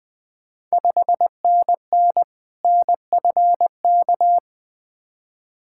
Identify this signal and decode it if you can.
Morse code, heard as a keyed tone, about 20 wpm, 700 Hz. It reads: 5NN NFK
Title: Ontario QSO Party Exchange 20wpm